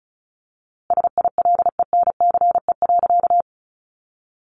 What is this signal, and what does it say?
Morse code, heard as a keyed tone, about 34 wpm, 700 Hz. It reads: SILENCE.
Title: A Sound Of Thunder 35wpm